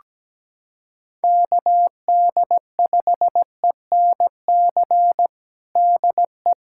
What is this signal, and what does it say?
Morse code, heard as a keyed tone, about 17 wpm, 700 Hz. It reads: KD5ENC DE
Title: USA Callsign DE Callsign - 6 Characters - 17wpm